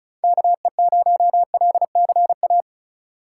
Morse code, heard as a keyed tone, about 46 wpm, 700 Hz. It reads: KE0LCA